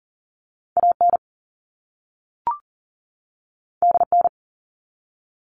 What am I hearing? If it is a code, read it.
Morse code, heard as a keyed tone, about 40 wpm, 700 Hz. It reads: AN DN
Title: All Cut Numbers - 2-Digit 40wpm